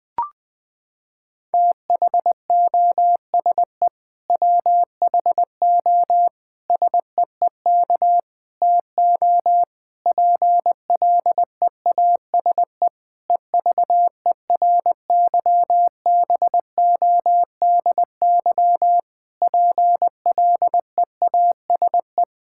Morse code, heard as a keyed tone, about 20 wpm, 700 Hz. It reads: THOSE WHO SEEK TO PLEASE EVERYBODY PLEASE
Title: Sentences from Top 1600 Words 20wpm